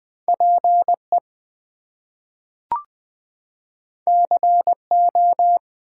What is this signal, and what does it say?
Morse code, heard as a keyed tone, about 20 wpm, 700 Hz. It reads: PE CO
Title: US States and Canadian Provinces 20wpm